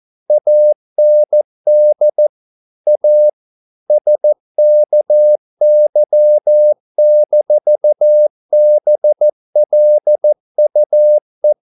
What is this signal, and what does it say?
Morse code, heard as a keyed tone, about 14 wpm, 600 Hz. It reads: AND A SKY-BLUE